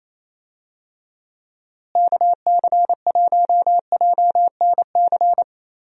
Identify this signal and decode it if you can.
Morse code, heard as a keyed tone, about 28 wpm, 700 Hz. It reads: KC1JNC
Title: USA Call Signs - 6 Characters - Repeated 2x - 28wpm